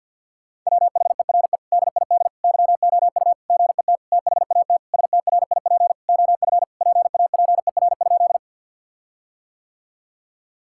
Morse code, heard as a keyed tone, about 50 wpm, 700 Hz. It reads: WHERE DID YOU GET THAT STRIP OF PAPER?
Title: Sentences from Top 1600 Words 50wpm